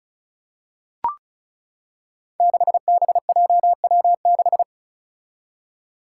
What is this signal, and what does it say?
Morse code, heard as a keyed tone, about 36 wpm, 700 Hz. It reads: 6BJW6
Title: Five Challenging Letter-Number Combinations 35wpm